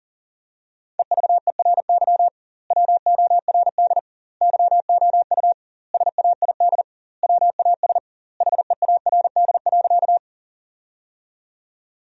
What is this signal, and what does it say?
Morse code, heard as a keyed tone, about 37 wpm, 700 Hz. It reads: EVERY WORD YOU SAID WAS HEARD.
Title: 2x Word-Spacing - Sentences from Top 400 Words - 40wpm